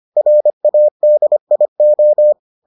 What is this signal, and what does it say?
Morse code, heard as a keyed tone, about 25 wpm, 600 Hz. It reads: RADIO